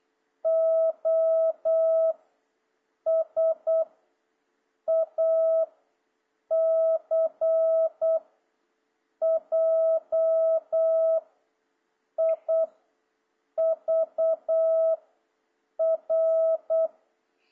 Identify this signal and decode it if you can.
Morse code, heard as a keyed tone, about 8 wpm, 660 Hz. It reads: O S A C J I V R
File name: jh8cbh / 2026cw